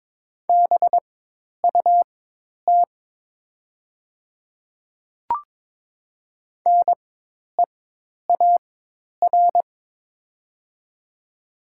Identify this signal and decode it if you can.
Morse code, heard as a keyed tone, about 23 wpm, spaced as for 10 wpm, 700 Hz. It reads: BUT NEAR
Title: 4x Character Spacing - Top 200 Words - 22wpm